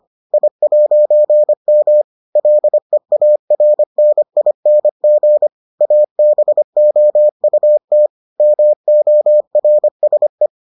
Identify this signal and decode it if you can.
Morse code, heard as a keyed tone, about 25 wpm, 600 Hz. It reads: I'M LEARNING ABOUT MORSE